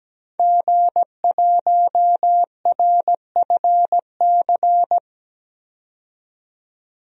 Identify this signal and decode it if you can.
Morse code, heard as a keyed tone, about 17 wpm, 700 Hz. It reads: G1RFC
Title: DXCC-100 Call Signs - 5 Characters - 17wpm